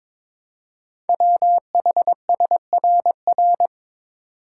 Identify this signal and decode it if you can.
Morse code, heard as a keyed tone, about 22 wpm, 700 Hz. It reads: WHSRR
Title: Code Groups - 5 Letters - 22wpm